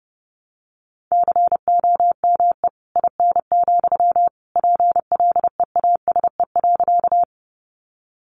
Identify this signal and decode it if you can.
Morse code, heard as a keyed tone, about 30 wpm, 700 Hz. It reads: COME IN, PLEASE.
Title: Sentences from Top 1500 Words 30wpm